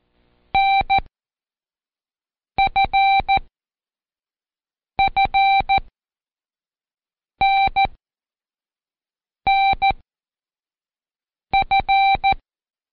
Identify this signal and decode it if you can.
Morse code, heard as a keyed tone, about 14 wpm, spaced as for 5 wpm, 780 Hz. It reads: NFFNNF
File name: Code Quick / AudioDownld1 / Seg43-Day5-Intro Z 1